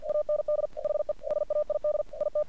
Morse code, heard as a keyed tone, about 49 wpm, 600 Hz. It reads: AND HE SAID IT
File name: archived / CRSnet